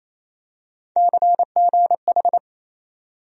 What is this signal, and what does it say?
Morse code, heard as a keyed tone, about 28 wpm, 700 Hz. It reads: CGH
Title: USA Call Sign Postfixes - 3 Letters 28wpm